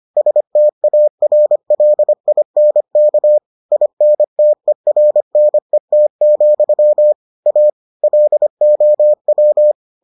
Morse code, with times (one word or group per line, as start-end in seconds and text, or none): 0.2-3.4 STARLINK
3.7-7.1 INTERNET,
7.5-7.7 A
8.0-9.7 LOW